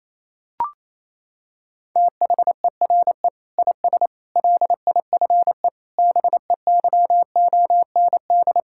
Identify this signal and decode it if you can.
Morse code, heard as a keyed tone, about 28 wpm, 700 Hz. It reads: THERE IS LIFE BEYOND
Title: Sentences from Top 1000 Words - Review Part 6 - 28wpm